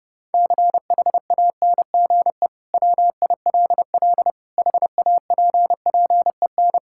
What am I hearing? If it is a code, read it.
Morse code, heard as a keyed tone, about 30 wpm, 700 Hz. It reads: CHANGE WILL HAPPEN
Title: Sentences from Top 1000 Words - Review Part 5 - 30wpm